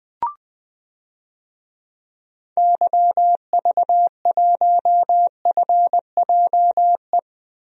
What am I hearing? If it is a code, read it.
Morse code, heard as a keyed tone, about 20 wpm, 700 Hz. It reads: YV1FJE
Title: DXCC-100 Call Signs - 6 Characters - Repeated 3x - 20wpm